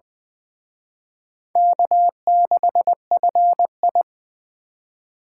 Morse code, heard as a keyed tone, about 24 wpm, 700 Hz. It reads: K6FI
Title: Speed Racing - USA 1x2 Call Signs 20wpm